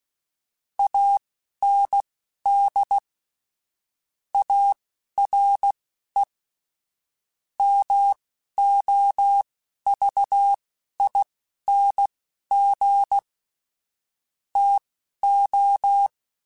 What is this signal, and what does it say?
Morse code, heard as a keyed tone, about 16 wpm, spaced as for 11 wpm, 800 Hz. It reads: AND ARE MOVING TO
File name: QSOs / QSOs-mp3s / WD0BGP 1@13